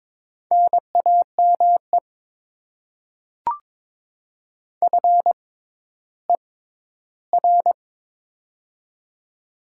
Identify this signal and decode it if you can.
Morse code, heard as a keyed tone, about 22 wpm, spaced as for 13 wpm, 700 Hz. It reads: NAME FER
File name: Top 20 QSO Elements - 6x Character Spacing - 22wpm